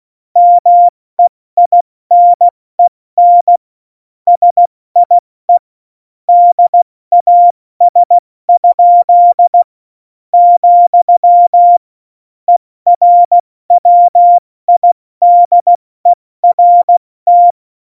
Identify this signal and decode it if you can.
Morse code, heard as a keyed tone, about 16 wpm, spaced as for 14 wpm, 700 Hz. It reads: MEINEN SIE DAS?, ERWIDERT
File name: Morsekurs / Witze / Witz 29. September 2020, 70 F80